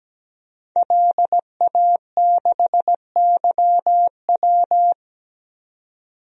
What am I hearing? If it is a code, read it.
Morse code, heard as a keyed tone, about 17 wpm, 700 Hz. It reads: LA6YW
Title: DXCC-100 Call Signs - 5 Characters - 17wpm